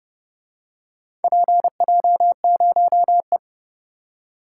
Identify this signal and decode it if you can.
Morse code, heard as a keyed tone, about 30 wpm, 700 Hz. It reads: PJ0E